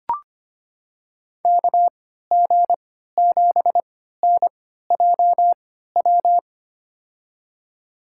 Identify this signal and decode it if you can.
Morse code, heard as a keyed tone, about 25 wpm, spaced as for 14 wpm, 700 Hz. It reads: KG7NJW
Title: USA Call Signs - 6 Characters - 3x Character Spacing - 25wpm